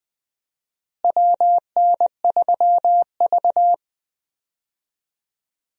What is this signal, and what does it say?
Morse code, heard as a keyed tone, about 20 wpm, 700 Hz. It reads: WN3V
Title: USA Call Signs - 4 Characters - Repeated 3x - 20wpm